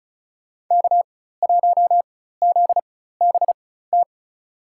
Morse code, heard as a keyed tone, about 34 wpm, spaced as for 16 wpm, 700 Hz. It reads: K1ZBT
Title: USA Call Signs - 5 Characters - 4x Character Spacing - 35wpm